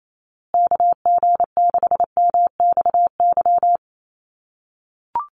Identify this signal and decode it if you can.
Morse code, heard as a keyed tone, about 28 wpm, 690 Hz. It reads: KG6MXY
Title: USA Call Signs Repeated 3x 28wpm